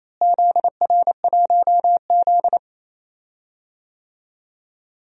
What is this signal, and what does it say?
Morse code, heard as a keyed tone, about 28 wpm, 700 Hz. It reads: ZR1Z
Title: DXCC-100 Call Signs - 4 Characters - Repeated 3x - 28wpm